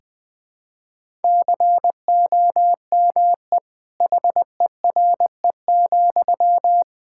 Morse code, heard as a keyed tone, about 20 wpm, 700 Hz. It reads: COME HERE,
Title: Sentences from Top 1600 Words 20wpm